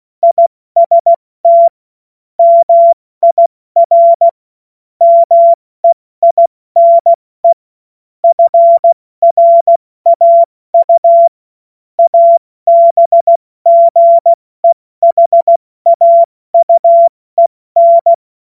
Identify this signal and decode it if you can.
Morse code, heard as a keyed tone, about 16 wpm, 680 Hz. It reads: IST MIR MEINE FRAU ABGEHAUEN
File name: Morsekurs / Witze / Witz 22. Februar 2022 70 F80